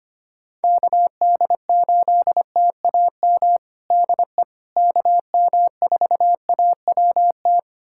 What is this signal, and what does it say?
Morse code, heard as a keyed tone, about 25 wpm, 700 Hz. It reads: KD8TAM DE KM4AWT
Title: USA Callsign DE Callsign - 6 Characters - 25wpm